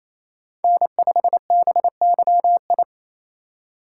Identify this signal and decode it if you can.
Morse code, heard as a keyed tone, about 28 wpm, 700 Hz. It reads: N5BYI
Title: USA Call Signs - 3x Character Spacing - 28wpm